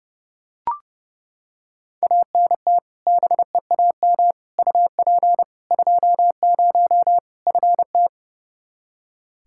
Morse code, heard as a keyed tone, about 30 wpm, 700 Hz. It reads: ANT BEAM UP 20 FT
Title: Practice QSO - Exchange 2 - Info - 30wpm